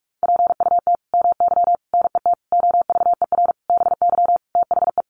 Morse code, heard as a keyed tone, about 44 wpm, 700 Hz. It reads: PUT MY NET OVER BY THE
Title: Sentences from Top 1400 Words 45wpm